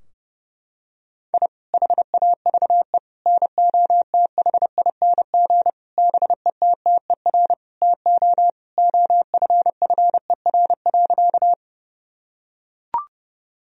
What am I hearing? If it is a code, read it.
Morse code, heard as a keyed tone, about 30 wpm, 700 Hz. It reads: I HAVE NOTHING BETTER TO OFFER.
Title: Sentences from Top 1000 Words - Review Part 4 - 30wpm